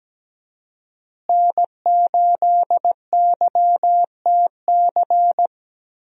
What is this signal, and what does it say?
Morse code, heard as a keyed tone, about 17 wpm, 700 Hz. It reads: N8YTC
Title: USA Call Signs - 5 Characters - Repeated 2x - 17wpm